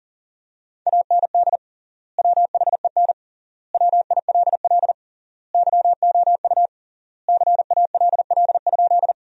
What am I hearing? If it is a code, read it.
Morse code, heard as a keyed tone, about 40 wpm, 700 Hz. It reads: AND WHEN WILL YOU CALL?
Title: 3x Word-Spacing - Sentences from Top 100 Words - 40wpm